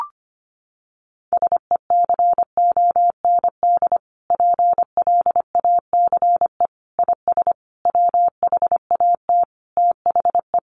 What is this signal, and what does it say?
Morse code, heard as a keyed tone, about 25 wpm, 700 Hz. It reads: SECOND PLACE IS WHAT THE
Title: Sentences from Top 1500 Words 25wpm